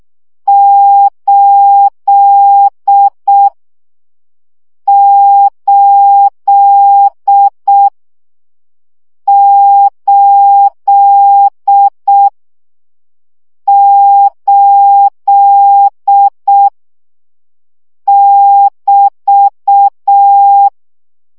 Morse code, heard as a keyed tone, about 6 wpm, 800 Hz. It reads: 8 8 8 8 <BT>
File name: CW gyakorlas 8 5 baud